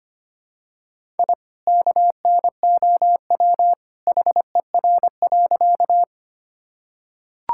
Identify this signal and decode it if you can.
Morse code, heard as a keyed tone, about 25 wpm, 700 Hz. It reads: I KNOW HER.
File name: practice / 25wpm / Sentences with 2-Word Phrases - 25wpm